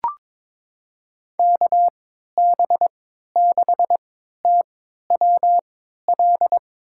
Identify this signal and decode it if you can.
Morse code, heard as a keyed tone, about 22 wpm, spaced as for 13 wpm, 700 Hz. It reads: KB6TWL
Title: USA Call Signs - 6 Characters - 3x Character Spacing - 22wpm